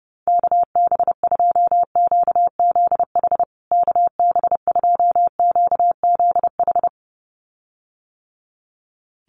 Morse code, heard as a keyed tone, about 30 wpm, 700 Hz. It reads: KB2QZH KB2QZH
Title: USA Call Signs Repeated 2x 30wpm